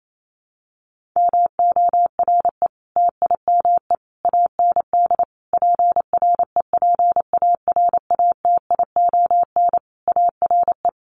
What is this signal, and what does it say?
Morse code, heard as a keyed tone, about 28 wpm, 700 Hz. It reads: MORE TIME AND PREPARATION ARE
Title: Sentences from Top 1400 Words 28wpm